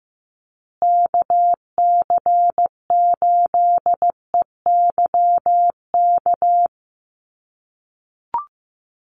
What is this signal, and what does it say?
Morse code, heard as a keyed tone, about 15 wpm, 700 Hz. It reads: KC8EYK
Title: USA Call Signs Repeated 2x 15wpm